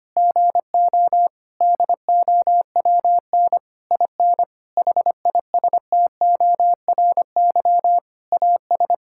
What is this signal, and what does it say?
Morse code, heard as a keyed tone, about 25 wpm, 700 Hz. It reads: GO DOWN IN HISTORY AS
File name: practice / 25wpm / Sentences with 9-Word Phrases - 25wpm